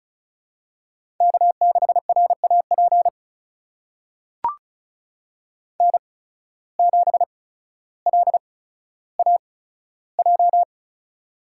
Morse code, heard as a keyed tone, about 36 wpm, spaced as for 15 wpm, 700 Hz. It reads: K6RAP N7LAJ
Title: USA Call Signs - 5 Characters - 8x Character Spacing - 35wpm